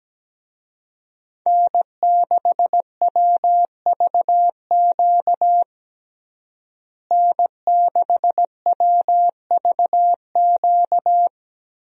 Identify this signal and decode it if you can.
Morse code, heard as a keyed tone, about 17 wpm, 700 Hz. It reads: N6WVQ N6WVQ
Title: USA Call Signs - 5 Characters - Repeated 3x - 17wpm